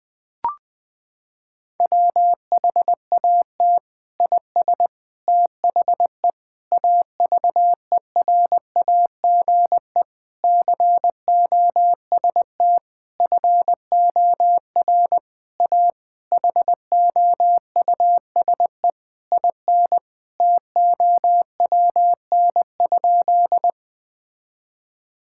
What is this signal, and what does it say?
Morse code, heard as a keyed tone, about 21 wpm, 700 Hz. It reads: WHAT IS THE AVERAGE COST FOR A HOUSE IN TOWN?
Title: Sentences from Top 1000 Words - Review Part 6 - 20wpm